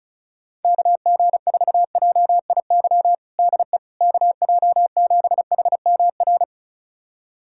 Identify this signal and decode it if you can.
Morse code, heard as a keyed tone, about 36 wpm, 700 Hz. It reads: KG4JIY DE KJ7HMR
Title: USA Callsign DE Callsign 35wpm